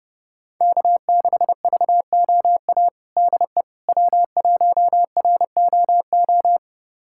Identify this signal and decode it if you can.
Morse code, heard as a keyed tone, about 30 wpm, 700 Hz. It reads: K6VOA DE W1ROO
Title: USA Callsign DE Callsign - 5 Characters - 30wpm